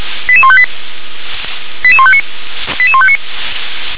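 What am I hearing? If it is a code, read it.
Five-tone selective call (ZVEI), heard as quick runs of five tones, tones of about 70 ms. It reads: 88158 88159 88159
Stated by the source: ZVEI five-tone selective call 88119 received via SDR.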